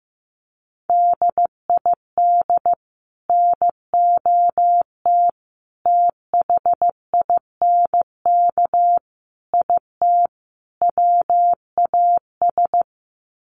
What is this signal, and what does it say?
Morse code, heard as a keyed tone, about 15 wpm, 700 Hz. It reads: DID NOT THINK IT WAS
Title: Sets of 5 Words Encore 15wpm